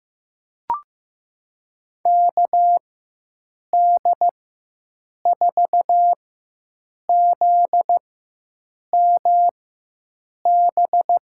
Morse code, heard as a keyed tone, about 15 wpm, spaced as for 7 wpm, 700 Hz. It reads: KD4ZMB
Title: USA Call Signs - 4x Character Spacing - 15wpm